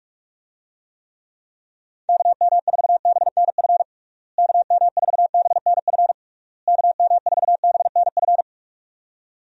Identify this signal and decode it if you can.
Morse code, heard as a keyed tone, about 46 wpm, 700 Hz. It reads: KM4BNF KM4BNF KM4BNF
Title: USA Call Signs - 6 Characters - Repeated 3x - 45wpm